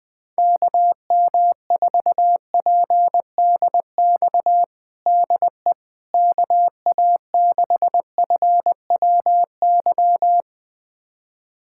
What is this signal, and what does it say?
Morse code, heard as a keyed tone, about 20 wpm, 700 Hz. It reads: KM4PDX DE KA6FWY
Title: USA Callsign DE Callsign - 6 Characters - 20wpm